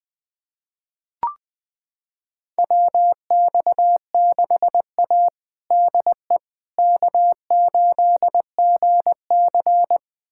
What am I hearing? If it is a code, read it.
Morse code, heard as a keyed tone, about 21 wpm, 700 Hz. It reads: WX6A DE K8GC
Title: USA Callsign DE Callsign - 4 Characters - 20wpm